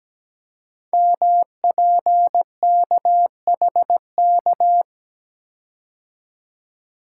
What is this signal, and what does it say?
Morse code, heard as a keyed tone, about 17 wpm, 700 Hz. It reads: MPKHK
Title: Code Groups - 5 Letter:Numbers - 17wpm